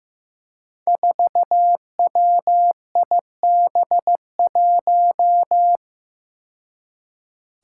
Morse code, heard as a keyed tone, about 15 wpm, 700 Hz. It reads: 4WIB1
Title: Five Challenging Letter-Number Combinations 15wpm